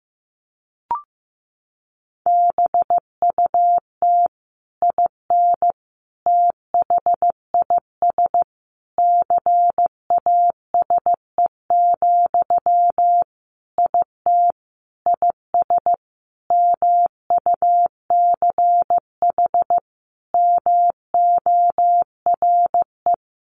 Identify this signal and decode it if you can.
Morse code, heard as a keyed tone, about 15 wpm, 700 Hz. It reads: BUT IN THIS CASE, IT IS MUCH MORE
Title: Sentences from Top 1200 Words 15wpm